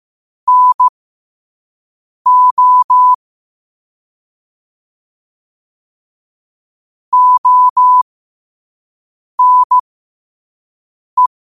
Morse code, heard as a keyed tone, about 15 wpm, spaced as for 5 wpm, 1000 Hz. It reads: NO ONE
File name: qotdfiles / 5wpm